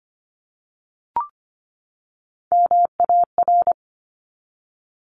Morse code, heard as a keyed tone, about 25 wpm, 700 Hz. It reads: MAR
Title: ARRL Sections 25wpm